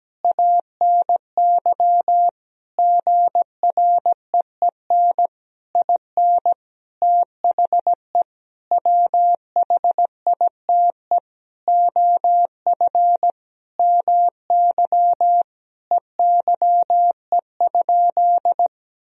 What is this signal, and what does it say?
Morse code, heard as a keyed tone, about 17 wpm, 700 Hz. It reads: ANY GREEN IN THE WHITE OF MY EYE?
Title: Sentences from Top 500 Words and Abbreviations 17wpm